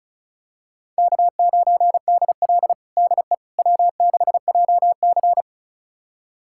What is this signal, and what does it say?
Morse code, heard as a keyed tone, about 36 wpm, 700 Hz. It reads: K9DL DE W6JC
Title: USA Callsign DE Callsign - 4 Characters - 35wpm